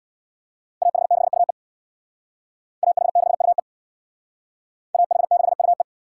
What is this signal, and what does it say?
Morse code, heard as a keyed tone, about 61 wpm, 700 Hz. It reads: AH6UE AH6UE AH6UE
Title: Speed Racing - USA 2x2 Call Signs 50wpm